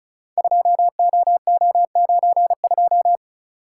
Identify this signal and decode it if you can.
Morse code, heard as a keyed tone, about 36 wpm, 700 Hz. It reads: 2OO92
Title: Code Groups - 5 Letter:Numbers - 35wpm